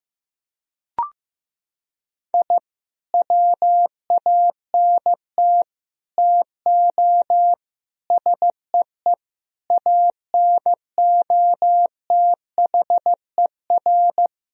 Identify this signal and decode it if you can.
Morse code, heard as a keyed tone, about 15 wpm, 700 Hz. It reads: I WANT TO SEE ANOTHER
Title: Sentences from Top 1000 Words - Review Part 1 - 15wpm